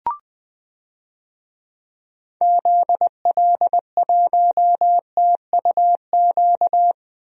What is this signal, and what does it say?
Morse code, heard as a keyed tone, about 20 wpm, 700 Hz. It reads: ZL1TUQ
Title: DXCC-100 Call Signs - 6 Characters - 20wpm